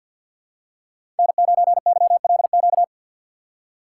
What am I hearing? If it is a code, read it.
Morse code, heard as a keyed tone, about 52 wpm, 700 Hz. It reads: N9YLQ
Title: USA Call Signs - 5 Characters - 8x Character Spacing - 50wpm